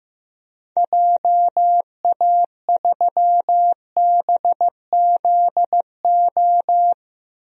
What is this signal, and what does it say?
Morse code, heard as a keyed tone, about 15 wpm, 700 Hz. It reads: JA3BZO
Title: DXCC-100 Call Signs - 6 Characters - 15wpm